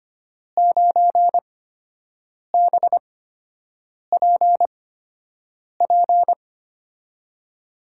Morse code, heard as a keyed tone, about 25 wpm, spaced as for 7 wpm, 700 Hz. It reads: 9BPP
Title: USA Call Signs - 6 Characters - 8x Character Spacing - 25wpm